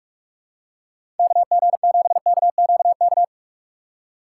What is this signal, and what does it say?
Morse code, heard as a keyed tone, about 46 wpm, 700 Hz. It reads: KG7KQK
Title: USA Call Signs - 6x Character Spacing - 45wpm